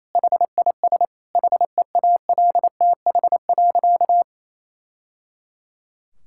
Morse code, heard as a keyed tone, about 27 wpm, 700 Hz. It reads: HIS HEALTH.
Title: Sentences from Top 1000 Words - Review Part 6 - 28wpm